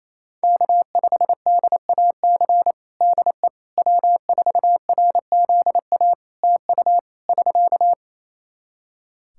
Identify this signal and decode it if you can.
Morse code, heard as a keyed tone, about 28 wpm, 700 Hz. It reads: K5DAC DE W4RZA TU <SK>
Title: Practice QSO - Exchange 4 - Ending - 28wpm